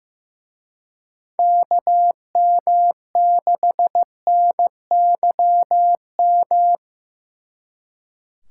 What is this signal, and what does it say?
Morse code, heard as a keyed tone, about 18 wpm, 700 Hz. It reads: KM6NYM
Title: Speed Racing - USA 2x3 Call Signs 15wpm